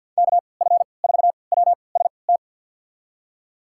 Morse code, heard as a keyed tone, about 52 wpm, spaced as for 29 wpm, 700 Hz. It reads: KF4WST
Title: USA Call Signs - 6 Characters - 3x Character Spacing - 50wpm